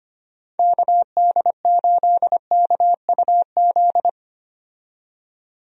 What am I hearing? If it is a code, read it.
Morse code, heard as a keyed tone, about 25 wpm, 700 Hz. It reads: KD8KUZ
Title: USA Call Signs - 6 Characters - 25wpm